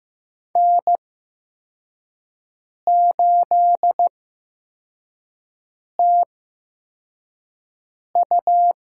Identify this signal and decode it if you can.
Morse code, heard as a keyed tone, about 15 wpm, spaced as for 4 wpm, 700 Hz. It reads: N8TU
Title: USA Call Signs - 5 Characters - 8x Character Spacing - 15wpm